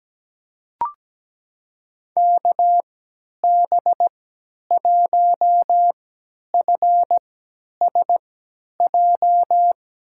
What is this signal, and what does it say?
Morse code, heard as a keyed tone, about 17 wpm, spaced as for 10 wpm, 700 Hz. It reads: KB1FSJ
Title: USA Call Signs - 3x Character Spacing - 17wpm